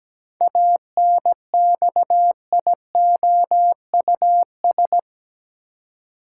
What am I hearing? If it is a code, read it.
Morse code, heard as a keyed tone, about 17 wpm, 700 Hz. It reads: ANXIOUS
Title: Seven Letter Words - 17wpm